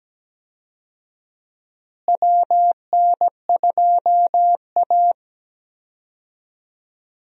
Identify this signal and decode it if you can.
Morse code, heard as a keyed tone, about 17 wpm, 700 Hz. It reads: WN2A
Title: USA Call Signs - 4 Characters - Repeated 2x - 17wpm